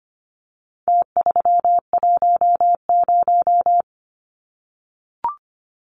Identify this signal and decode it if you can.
Morse code, heard as a keyed tone, about 25 wpm, 700 Hz. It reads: T310
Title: Numbers - Contest Style - 4-Digit 25wpm